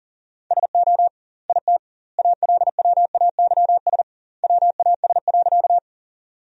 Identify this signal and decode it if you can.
Morse code, heard as a keyed tone, about 40 wpm, 700 Hz. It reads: SO IT ALWAYS WAS.
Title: 2x Word-Spacing - Sentences from Top 300 Words - 40wpm